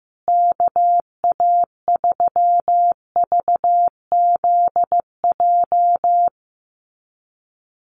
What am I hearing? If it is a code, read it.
Morse code, heard as a keyed tone, about 15 wpm, 700 Hz. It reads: KA3VZJ
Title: USA Call Signs Repeated 3x 15wpm